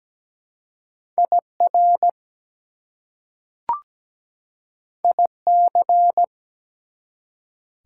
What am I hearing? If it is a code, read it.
Morse code, heard as a keyed tone, about 17 wpm, 700 Hz. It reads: IR IC
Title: Prefixes and Suffixes 17wpm